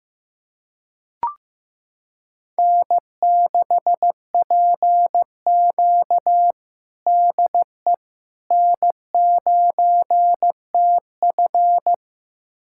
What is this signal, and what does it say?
Morse code, heard as a keyed tone, about 15 wpm, 700 Hz. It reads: N6PQ DE N9TF
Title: USA Callsign DE Callsign - 4 Characters - 15wpm